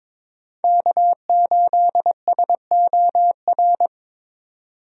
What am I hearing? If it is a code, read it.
Morse code, heard as a keyed tone, about 22 wpm, 700 Hz. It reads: K8SOR
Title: USA Call Signs - 5 Characters - 22wpm